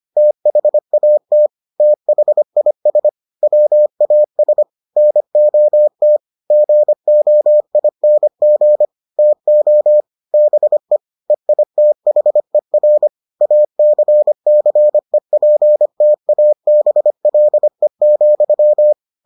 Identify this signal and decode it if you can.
Morse code, heard as a keyed tone, about 25 wpm, 600 Hz. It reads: THAT THIS WAS NOT GOING TO BE EITHER ACCEPTABLE,